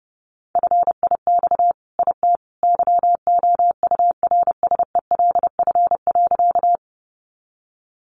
Morse code, heard as a keyed tone, about 30 wpm, 700 Hz. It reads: FIX IT YOURSELF.
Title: Sentences from Top 1400 Words 30wpm